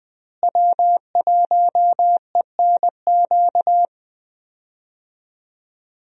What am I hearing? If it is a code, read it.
Morse code, heard as a keyed tone, about 20 wpm, 700 Hz. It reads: W1ENQ
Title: USA Call Signs - 5 Characters - Repeated 3x - 20wpm